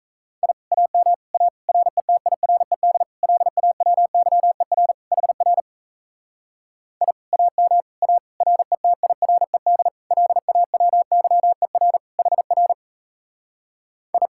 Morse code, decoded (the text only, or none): I AM A RETIRED LAWYER HR I AM A RETIRED LAWYER HR I